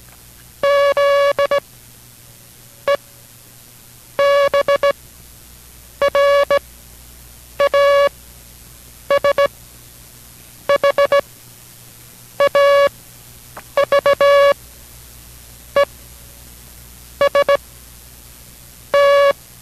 Morse code, heard as a keyed tone, about 17 wpm, spaced as for 6 wpm, 560 Hz. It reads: ZEBRAS HAVE ST